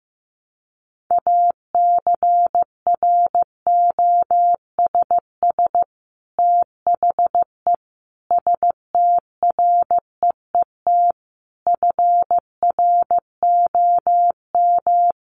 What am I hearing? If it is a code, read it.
Morse code, heard as a keyed tone, about 15 wpm, 700 Hz. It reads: ACROSS THE STREET FROM
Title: Sets of 5 Words Encore 15wpm